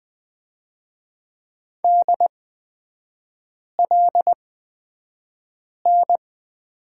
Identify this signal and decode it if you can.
Morse code, heard as a keyed tone, about 20 wpm, spaced as for 5 wpm, 700 Hz. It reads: DLN
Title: 236 Lesson 38e - Letter q - Three Characters 20wpm